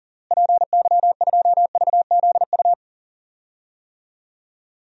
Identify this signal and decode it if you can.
Morse code, heard as a keyed tone, about 37 wpm, 700 Hz. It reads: PY2VZU
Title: DXCC-100 Call Signs - 6 Characters - Repeated 3x - 40wpm